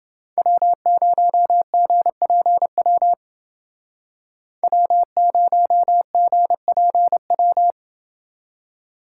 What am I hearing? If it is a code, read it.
Morse code, heard as a keyed tone, about 28 wpm, 700 Hz. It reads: W0GPW W0GPW